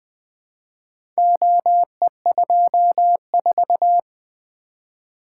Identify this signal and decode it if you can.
Morse code, heard as a keyed tone, about 20 wpm, 700 Hz. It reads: OE24
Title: Four Letter-Number Combinations - 20wpm